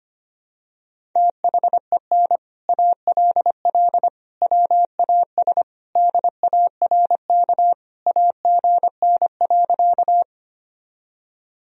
Morse code, heard as a keyed tone, about 25 wpm, 700 Hz. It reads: THEN ALL WAS DARK AGN.
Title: Sentences from Top 500 Words and Abbreviations 25wpm